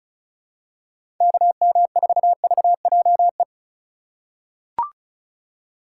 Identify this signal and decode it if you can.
Morse code, heard as a keyed tone, about 36 wpm, 700 Hz. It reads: KM4VJE
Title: USA Call Signs - 6 Characters - 6x Character Spacing - 35wpm